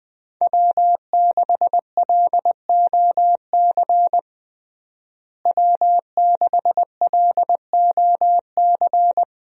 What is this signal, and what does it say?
Morse code, heard as a keyed tone, about 20 wpm, 700 Hz. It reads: W6LOC W6LOC
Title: USA Call Signs - 5 Characters - Repeated 2x - 20wpm